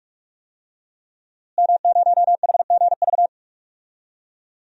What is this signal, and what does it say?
Morse code, heard as a keyed tone, about 41 wpm, 700 Hz. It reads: M0HGV
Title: DXCC-100 Call Signs - 5 Characters - Repeated 3x - 45wpm